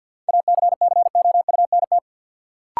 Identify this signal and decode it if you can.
Morse code, heard as a keyed tone, about 53 wpm, 700 Hz. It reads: ACCOUNT
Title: Sentences from Top 900 Words - New Word Drill - 50wpm